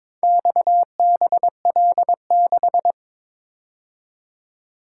Morse code, heard as a keyed tone, about 23 wpm, 700 Hz. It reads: XBL6
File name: practice / 22wpm / Four Letter-Number Combinations - 22wpm